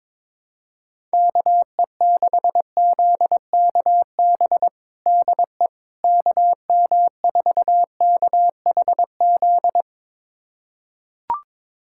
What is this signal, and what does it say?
Morse code, heard as a keyed tone, about 22 wpm, 700 Hz. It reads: KE6ZKB DE KM4KHZ
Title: USA Callsign DE Callsign 22wpm